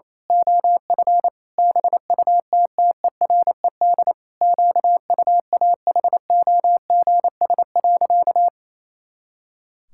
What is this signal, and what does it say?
Morse code, heard as a keyed tone, about 28 wpm, 700 Hz. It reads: OF BUTTERED QUAHOGS.
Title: Pangram Challenge v2 - 28wpm